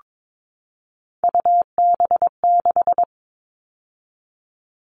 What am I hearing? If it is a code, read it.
Morse code, heard as a keyed tone, about 22 wpm, 700 Hz. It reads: UB6
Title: All Cut Numbers - 3-Digit 22wpm